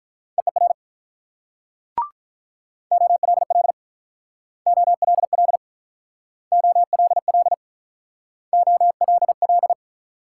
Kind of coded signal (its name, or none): Morse code